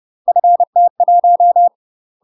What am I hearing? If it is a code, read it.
Morse code, heard as a keyed tone, about 30 wpm, 700 Hz. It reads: FT1
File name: UK MASTER QSOs TRAINING / 30WPM / UK MASTER QSOs 0021